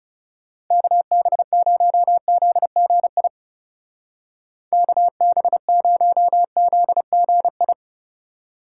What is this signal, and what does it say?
Morse code, heard as a keyed tone, about 35 wpm, 700 Hz. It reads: KB0ZGI KB0ZGI
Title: Speed Racing - USA 2x3 Call Signs 30wpm